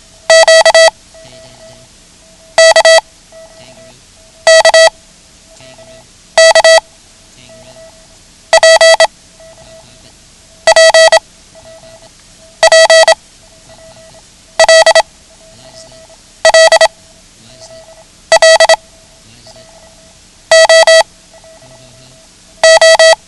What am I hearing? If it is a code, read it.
Morse code, heard as a keyed tone, about 27 wpm, 650 Hz. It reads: Q K K K P P P L L L O O
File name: ham radio / CodeQuick / Code Quick 09